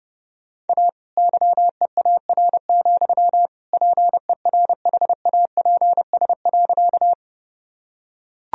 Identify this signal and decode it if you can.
Morse code, heard as a keyed tone, about 32 wpm, 700 Hz. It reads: A YEAR, PERHAPS.
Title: Sentences from Top 1000 Words - Review Part 1 - 30wpm